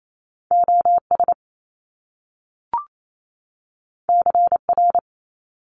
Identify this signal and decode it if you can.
Morse code, heard as a keyed tone, about 28 wpm, 700 Hz. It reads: OS CR
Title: Top Two Letter Combinations in Words 28wpm